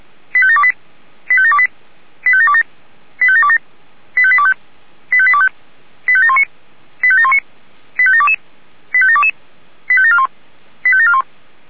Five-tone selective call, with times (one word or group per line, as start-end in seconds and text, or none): 0.3-0.7 85728
1.3-1.7 85728
2.3-2.6 85727
3.2-3.6 85727
4.2-4.5 85825
5.1-5.5 85825
6.1-6.4 85719
7.0-7.4 85719
8.0-8.4 85722
8.9-9.3 85722
9.9-10.3 85741
10.8-11.2 85741